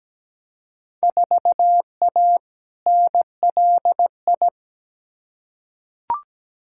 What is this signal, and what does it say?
Morse code, heard as a keyed tone, about 17 wpm, 700 Hz. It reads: 4A NLI
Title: Field Day Exchanges Encore - 17wpm